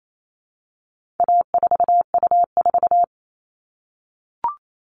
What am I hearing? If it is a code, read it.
Morse code, heard as a keyed tone, about 28 wpm, 700 Hz. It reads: A4U4
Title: All Cut Numbers - 4-Digit 28wpm